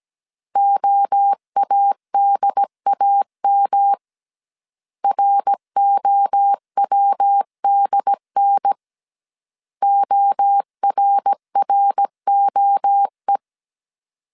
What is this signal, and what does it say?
Morse code, heard as a keyed tone, about 17 wpm, 800 Hz. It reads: OADAM ROWDN ORROE